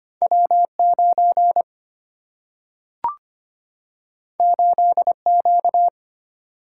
Morse code, heard as a keyed tone, about 25 wpm, 690 Hz. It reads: W9 8Q
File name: Two Letter-Number Combinations - 25wpm